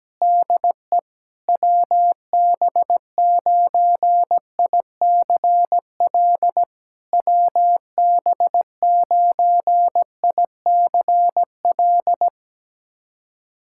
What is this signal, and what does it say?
Morse code, heard as a keyed tone, about 17 wpm, 700 Hz. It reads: DE WB9ICL WB9ICL
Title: SOTA QSO Practice 17wpm